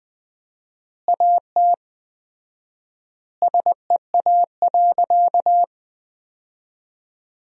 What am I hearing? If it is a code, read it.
Morse code, heard as a keyed tone, about 20 wpm, 700 Hz. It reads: AT SEA.
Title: 4x Word-Spacing - Sentences from Top 300 Words - 20wpm